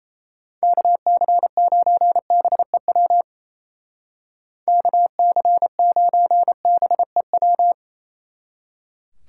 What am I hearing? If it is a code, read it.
Morse code, heard as a keyed tone, about 35 wpm, 700 Hz. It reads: KC9BEW KC9BEW
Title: Speed Racing - USA 2x3 Call Signs 28wpm